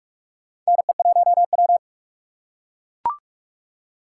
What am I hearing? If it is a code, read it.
Morse code, heard as a keyed tone, about 42 wpm, 700 Hz. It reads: NE1W
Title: USA Call Signs - 4 Characters - 2x Character Spacing - 45wpm